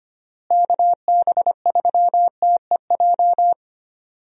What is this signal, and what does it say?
Morse code, heard as a keyed tone, about 25 wpm, 700 Hz. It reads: KB3TEJ
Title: USA Call Signs - 6 Characters - Repeated 2x - 25wpm